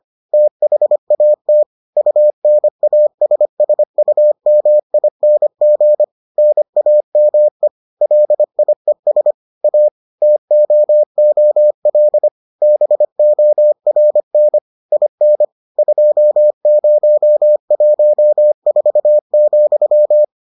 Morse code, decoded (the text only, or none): THAT UNASSUMING NAME LIES A TOOL BORN IN 2014,